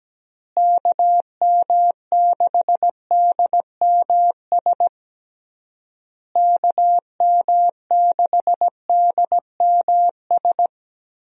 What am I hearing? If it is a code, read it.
Morse code, heard as a keyed tone, about 17 wpm, 700 Hz. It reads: KM6DMS KM6DMS
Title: USA Call Signs - 6 Characters - Repeated 3x - 17wpm